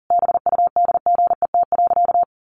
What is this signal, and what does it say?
Morse code, heard as a keyed tone, about 40 wpm, 700 Hz. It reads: BUDGET.
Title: Sentences from Top 1200 Words 40wpm